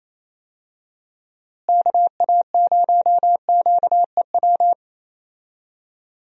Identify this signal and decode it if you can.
Morse code, heard as a keyed tone, about 28 wpm, 700 Hz. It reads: KA0QEW
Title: USA Call Signs - 6 Characters - Repeated 2x - 28wpm